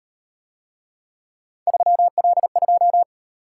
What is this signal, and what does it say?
Morse code, heard as a keyed tone, about 38 wpm, 700 Hz. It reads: 3L2